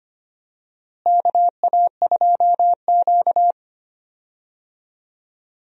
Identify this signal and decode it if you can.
Morse code, heard as a keyed tone, about 25 wpm, 700 Hz. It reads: KA2Q
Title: USA Call Signs - 4 Characters - Repeated 2x - 25wpm